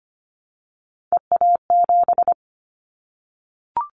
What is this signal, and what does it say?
Morse code, heard as a keyed tone, about 25 wpm, 700 Hz. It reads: EA7
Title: Callsign Prefixes for Top 25 DXCC - Repeated 3x - 25wpm